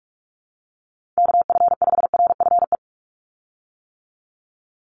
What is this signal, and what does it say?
Morse code, heard as a keyed tone, about 44 wpm, 700 Hz. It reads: KF5RFE
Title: USA Call Signs 45wpm